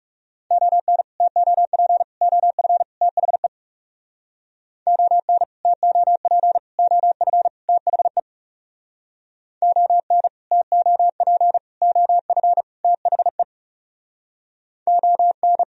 Morse code, heard as a keyed tone, about 37 wpm, 700 Hz. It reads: ON TOP OF THE ON TOP OF THE ON TOP OF THE ON
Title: Speed-Racing - Sets of 4 Words 30wpm